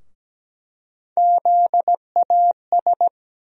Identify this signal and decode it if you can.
Morse code, heard as a keyed tone, about 17 wpm, 700 Hz. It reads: ZAS
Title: USA Call Sign Postfixes - 3 Letters 17wpm